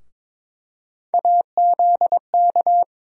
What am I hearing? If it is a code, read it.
Morse code, heard as a keyed tone, about 22 wpm, 700 Hz. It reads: AZK
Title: USA Call Sign Postfixes - 3 Letters 22wpm